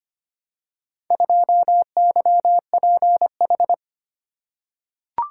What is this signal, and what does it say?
Morse code, heard as a keyed tone, about 25 wpm, 700 Hz. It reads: 2YPH
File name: practice / 25wpm / Four Letter-Number Combinations - 25wpm